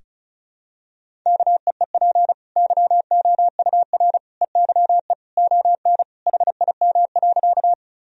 Morse code, heard as a keyed tone, about 36 wpm, 700 Hz. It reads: KEEP YOUR EYE ON HIM.
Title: Sentences from Top 1000 Words - Review Part 1 - 35wpm